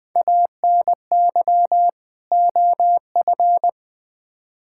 Morse code, heard as a keyed tone, about 20 wpm, 700 Hz. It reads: ANY OF
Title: Build A Phrase - 3-Word Phrases - 20wpm